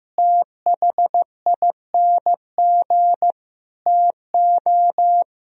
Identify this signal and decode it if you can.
Morse code, heard as a keyed tone, about 16 wpm, 700 Hz. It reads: THING TO
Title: Speed-Racing - Sets of 5 Words 15wpm